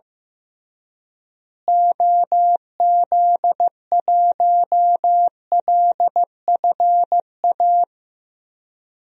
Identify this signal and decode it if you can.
Morse code, heard as a keyed tone, about 15 wpm, 700 Hz. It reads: OZ1LFA
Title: DXCC-100 Call Signs - 6 Characters - Repeated 3x - 15wpm